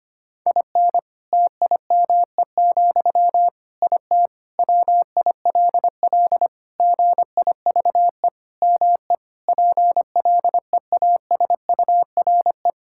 Morse code, heard as a keyed tone, about 25 wpm, 700 Hz. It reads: IN TIME, IT WILL GIVE ME PLEASURE
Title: Sentences from Top 1600 Words 25wpm